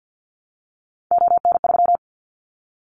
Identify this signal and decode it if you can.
Morse code, heard as a keyed tone, about 50 wpm, 700 Hz. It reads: ON3
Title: Callsign Prefixes for Top 50 DXCC - 50wpm